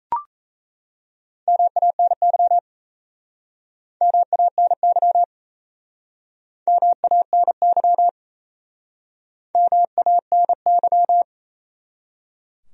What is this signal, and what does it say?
Morse code, heard as a keyed tone, about 38 wpm, 700 Hz. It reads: MANY MANY MANY MANY